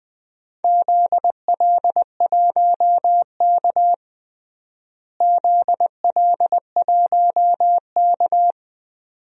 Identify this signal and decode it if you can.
Morse code, heard as a keyed tone, about 20 wpm, 700 Hz. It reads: ZL1K ZL1K
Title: DXCC-100 Call Signs - 4 Characters - Repeated 2x - 20wpm